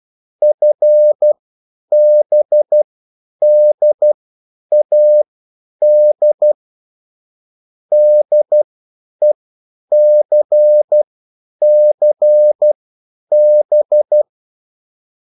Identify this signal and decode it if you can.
Morse code, heard as a keyed tone, about 12 wpm, spaced as for 9 wpm, 600 Hz. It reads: FBDAD DECCB